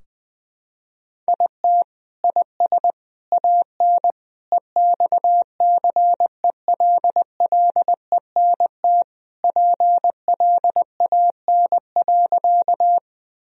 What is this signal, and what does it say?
Morse code, heard as a keyed tone, about 20 wpm, 700 Hz. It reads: IT IS AN EXCELLENT PLAN.
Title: Sentences from Top 1000 Words - Review Part 2 - 20wpm